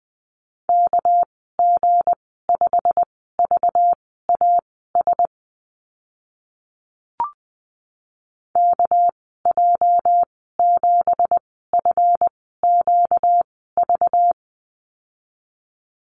Morse code, heard as a keyed tone, about 20 wpm, spaced as for 14 wpm, 700 Hz. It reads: KG5VAS KJ7FQV
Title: 238 Lesson 38g - Letter q - Call Signs 20wpm